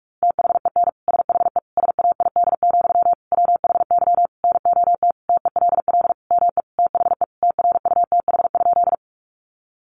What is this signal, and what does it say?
Morse code, heard as a keyed tone, about 44 wpm, 700 Hz. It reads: THEN SHE SAID, WHY NOT TELL ME THE TRUTH?
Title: Sentences from Top 800 Words 45wpm